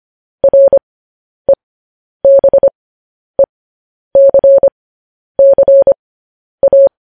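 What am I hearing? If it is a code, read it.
Morse code, heard as a keyed tone, about 25 wpm, spaced as for 10 wpm, 550 Hz. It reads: REBECCA
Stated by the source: Morse code transcription of news headlines.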